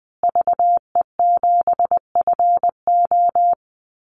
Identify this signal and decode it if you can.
Morse code, heard as a keyed tone, about 20 wpm, 700 Hz. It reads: VE7FO
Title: Call Signs Repeated 3x - 20wpm